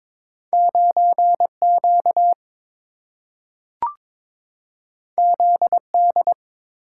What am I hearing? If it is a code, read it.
Morse code, heard as a keyed tone, about 22 wpm, 700 Hz. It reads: 9Q ZD
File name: practice / 22wpm / Two Letter-Number Combinations - 22wpm